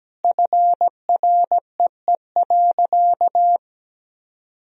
Morse code, heard as a keyed tone, about 17 wpm, 700 Hz. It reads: FREE.
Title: Sentences from Top 1000 Words - Review Part 1 - 17wpm